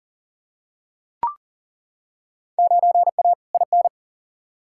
Morse code, heard as a keyed tone, about 40 wpm, 700 Hz. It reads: 9A IN
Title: Field Day Exchanges Encore - 40wpm